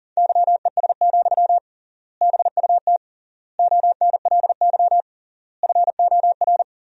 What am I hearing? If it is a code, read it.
Morse code, heard as a keyed tone, about 37 wpm, 700 Hz. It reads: YES, BUT ONLY FOR
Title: 3x Word-Spacing - Sentences from Top 500 Words - 40wpm